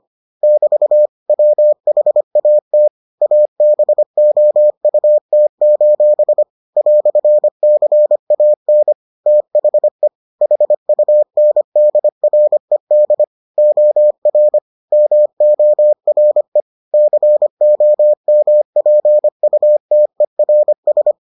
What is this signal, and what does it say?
Morse code, heard as a keyed tone, about 25 wpm, 600 Hz. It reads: <BT> WHAT ABOUT: "CAN THE HUNDRED OR MORE COMPUTERS